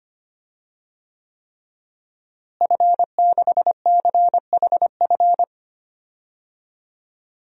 Morse code, heard as a keyed tone, about 25 wpm, 700 Hz. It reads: F6CHF
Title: DXCC-100 Call Signs - 5 Characters - Repeated 3x - 25wpm